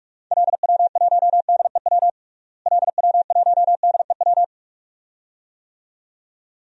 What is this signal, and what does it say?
Morse code, heard as a keyed tone, about 42 wpm, 700 Hz. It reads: LW1DEW LW1DEW
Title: DXCC-100 Call Signs - 6 Characters - Repeated 2x - 45wpm